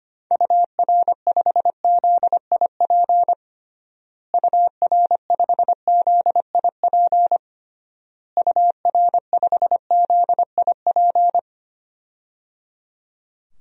Morse code, heard as a keyed tone, about 25 wpm, 700 Hz. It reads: UR5ZIP UR5ZIP UR5ZIP
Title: DXCC-100 Call Signs - 6 Characters - Repeated 3x - 25wpm